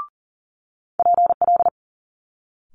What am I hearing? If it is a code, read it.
Morse code, heard as a keyed tone, about 38 wpm, 700 Hz. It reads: PL
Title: Multi-letter Phonemes Quick Fire 40wpm